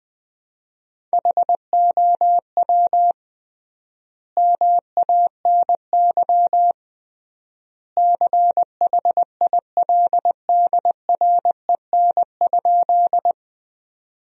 Morse code, heard as a keyed tone, about 20 wpm, 700 Hz. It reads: HOW MANY CHILDREN?
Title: 3x Word-Spacing - Sentences from Top 300 Words - 20wpm